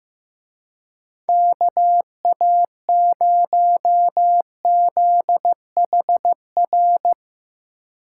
Morse code, heard as a keyed tone, about 16 wpm, 700 Hz. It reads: KA0ZHR
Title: Speed Racing - USA Call Signs 15wpm